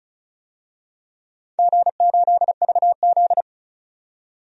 Morse code, heard as a keyed tone, about 34 wpm, 700 Hz. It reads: G8VZ
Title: DXCC-100 Call Signs - 4 Characters - 35wpm